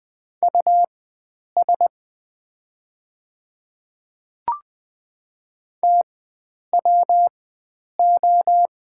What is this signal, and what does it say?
Morse code, heard as a keyed tone, about 20 wpm, spaced as for 9 wpm, 700 Hz. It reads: US TWO
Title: 4x Character Spacing - Top 300 Words - 20wpm